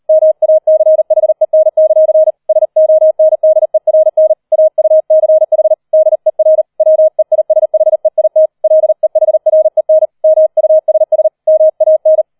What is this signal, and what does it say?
Morse code, heard as a keyed tone, about 38 wpm, 630 Hz. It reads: MACHEN; SONDERN AUCH DER WEISHEIT LEHREN MUSS MAN
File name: 004 Vierter Streich 40wpm